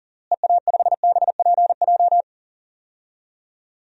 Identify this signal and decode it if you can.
Morse code, heard as a keyed tone, about 37 wpm, 700 Hz. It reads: EA5BPJ
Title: DXCC-100 Call Signs - 6 Characters - Repeated 3x - 40wpm